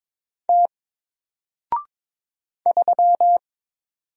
Morse code, heard as a keyed tone, about 22 wpm, 700 Hz. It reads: T 3
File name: Single-Letter-Number-22wpm